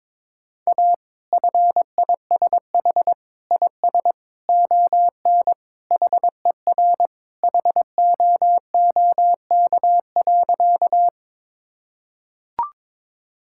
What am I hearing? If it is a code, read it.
Morse code, heard as a keyed tone, about 22 wpm, 700 Hz. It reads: A FISH IS ON HER HOOK.
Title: Sentences from Top 1600 Words 22wpm